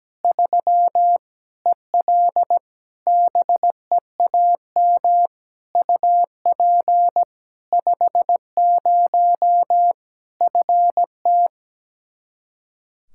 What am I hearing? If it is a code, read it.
Morse code, heard as a keyed tone, about 17 wpm, 700 Hz. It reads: 3 EL BEAM UP 50 FT
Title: K1USN SST QSO Practice 17wpm